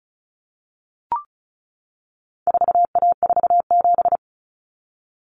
Morse code, heard as a keyed tone, about 34 wpm, 700 Hz. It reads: 4A47
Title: Common Cut Numbers - 4-Digit 35wpm